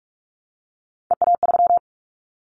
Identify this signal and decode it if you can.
Morse code, heard as a keyed tone, about 44 wpm, 700 Hz. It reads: EA3
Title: Callsign Prefixes for Top 25 DXCC - Repeated 3x - 45wpm